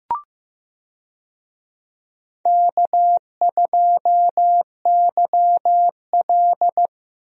Morse code, heard as a keyed tone, about 15 wpm, 700 Hz. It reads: K2YL
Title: USA Call Signs - 4 Characters - 15wpm